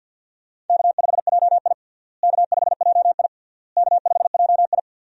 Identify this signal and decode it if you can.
Morse code, heard as a keyed tone, about 52 wpm, 700 Hz. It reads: K5JI K5JI K5JI
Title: USA Call Signs - 4 Characters - Repeated 3x - 50wpm